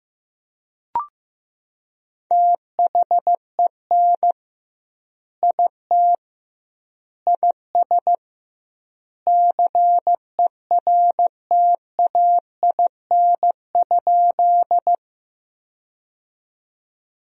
Morse code, heard as a keyed tone, about 16 wpm, 700 Hz. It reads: THEN IT IS CERTAIN?
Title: 2x Word-Spacing - Sentences from Top 500 Words - 15wpm